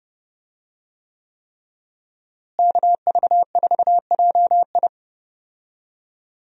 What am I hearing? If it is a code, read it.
Morse code, heard as a keyed tone, about 30 wpm, 700 Hz. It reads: KV4JI
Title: USA Call Signs - 5 Characters - Repeated 3x - 30wpm